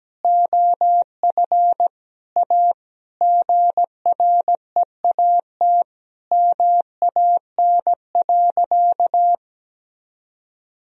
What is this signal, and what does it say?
Morse code, heard as a keyed tone, about 17 wpm, 700 Hz. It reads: OF A GREAT MAN.
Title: Sentences from Top 1000 Words - Review Part 2 - 17wpm